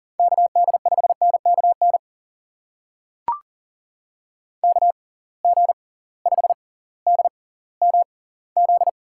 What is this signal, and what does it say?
Morse code, heard as a keyed tone, about 39 wpm, spaced as for 20 wpm, 700 Hz. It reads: KD5NKN KG5DMZ
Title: USA Call Signs - 6 Characters - 6x Character Spacing - 40wpm